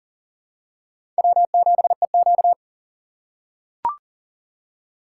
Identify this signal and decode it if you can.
Morse code, heard as a keyed tone, about 40 wpm, 700 Hz. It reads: W7EQ
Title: USA Call Signs - 4 Characters - 3x Character Spacing - 40wpm